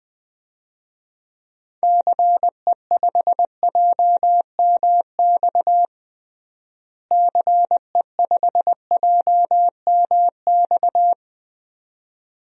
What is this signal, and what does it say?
Morse code, heard as a keyed tone, about 21 wpm, 700 Hz. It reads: CE5JMX CE5JMX
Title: DXCC-100 Call Signs - 6 Characters - Repeated 2x - 20wpm